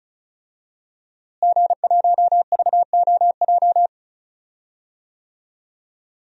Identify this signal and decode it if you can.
Morse code, heard as a keyed tone, about 34 wpm, 700 Hz. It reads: G1VOJ
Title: DXCC-100 Call Signs - 5 Characters - Repeated 3x - 35wpm